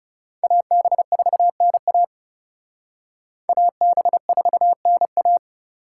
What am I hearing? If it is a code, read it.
Morse code, heard as a keyed tone, about 37 wpm, 700 Hz. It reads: AB4NA AB4NA
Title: Speed Racing - USA 2x2 Call Signs 30wpm